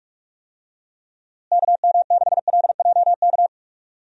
Morse code, heard as a keyed tone, about 46 wpm, 700 Hz. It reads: KM6LJK
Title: USA Call Signs - 6 Characters - Repeated 2x - 45wpm